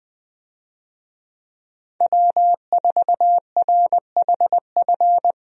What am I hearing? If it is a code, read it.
Morse code, heard as a keyed tone, about 20 wpm, 700 Hz. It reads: W4RHF
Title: USA Call Signs - 5 Characters - Repeated 3x - 20wpm